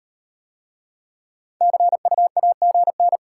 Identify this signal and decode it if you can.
Morse code, heard as a keyed tone, about 37 wpm, 690 Hz. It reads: CU AGN